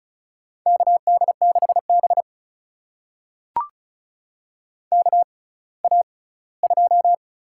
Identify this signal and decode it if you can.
Morse code, heard as a keyed tone, about 36 wpm, spaced as for 20 wpm, 700 Hz. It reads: KD6B KA2
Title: USA Call Signs - 6x Character Spacing - 35wpm